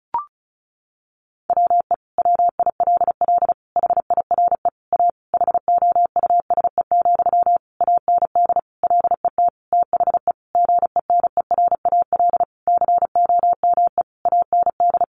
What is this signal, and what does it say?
Morse code, heard as a keyed tone, about 34 wpm, 700 Hz. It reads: WE WILL HIRE A HOUSE, AND LET THE GENERAL COME AND
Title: Sentences from Top 1400 Words 35wpm